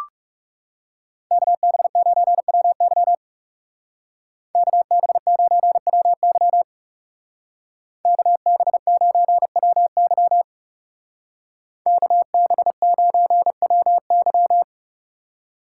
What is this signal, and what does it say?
Morse code, heard as a keyed tone, about 39 wpm, 700 Hz. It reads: KB9WY KB9WY KB9WY KB9WY